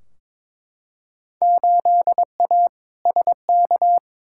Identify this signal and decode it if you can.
Morse code, heard as a keyed tone, about 22 wpm, 700 Hz. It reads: 8A SK
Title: Field Day Exchanges Encore - 22wpm